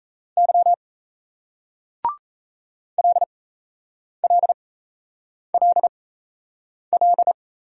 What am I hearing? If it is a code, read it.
Morse code, heard as a keyed tone, about 40 wpm, spaced as for 8 wpm, 700 Hz. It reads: Y LLLL